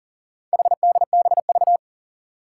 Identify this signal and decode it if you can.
Morse code, heard as a keyed tone, about 40 wpm, 700 Hz. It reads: HDBV
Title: Four Challenging Letter-Number Combinations 40wpm